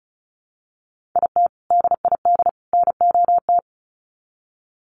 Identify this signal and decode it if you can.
Morse code, heard as a keyed tone, about 34 wpm, 700 Hz. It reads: IT DID NOT
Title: Sets of 3 Words 35wpm